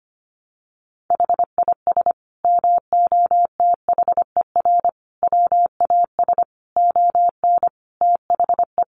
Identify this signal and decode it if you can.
Morse code, heard as a keyed tone, about 25 wpm, 700 Hz. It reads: HIS MOTHER WAS ON THE
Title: Sentences from Top 1200 Words 25wpm